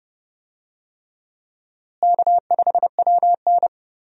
Morse code, heard as a keyed tone, about 30 wpm, 700 Hz. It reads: K5WN
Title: USA Call Signs - 4 Characters - Repeated 3x - 30wpm